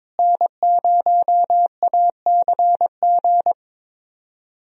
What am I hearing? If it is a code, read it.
Morse code, heard as a keyed tone, about 22 wpm, 700 Hz. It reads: N0ACG
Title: USA Call Signs - 5 Characters - Repeated 3x - 22wpm